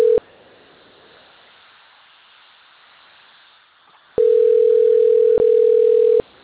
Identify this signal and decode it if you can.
Ringback tone (North America), 440 + 480 Hz.
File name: rg-1600-unknown- - - .86229